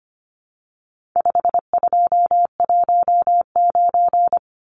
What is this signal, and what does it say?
Morse code, heard as a keyed tone, about 25 wpm, 700 Hz. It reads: 5219
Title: Numbers - 4-Digit 25wpm